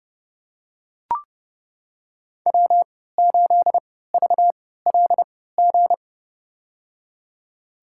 Morse code, heard as a keyed tone, about 32 wpm, spaced as for 17 wpm, 700 Hz. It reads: W8VLG
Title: USA Call Signs - 5 Characters - 3x Character Spacing - 30wpm